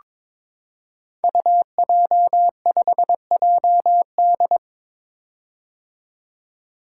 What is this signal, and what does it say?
Morse code, heard as a keyed tone, about 23 wpm, 700 Hz. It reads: UJ5JD
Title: Code Groups - 5 Letter:Numbers - 22wpm